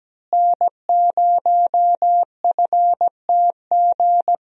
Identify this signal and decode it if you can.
Morse code, heard as a keyed tone, about 17 wpm, 700 Hz. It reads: N0FTG
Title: USA Call Signs - 5 Characters - Repeated 2x - 17wpm